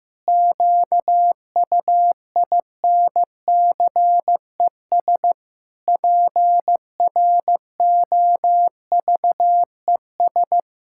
Morse code, heard as a keyed tone, about 15 wpm, 700 Hz. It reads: QUINCES PROVES
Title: Pangram Challenge v2 - 15wpm